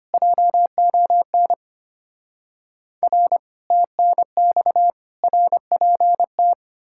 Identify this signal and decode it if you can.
Morse code, heard as a keyed tone, about 32 wpm, 700 Hz. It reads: JON R TNX RPT